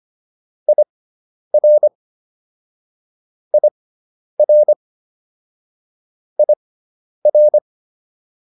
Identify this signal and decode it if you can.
Morse code, heard as a keyed tone, about 25 wpm, spaced as for 10 wpm, 600 Hz. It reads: IR IR IR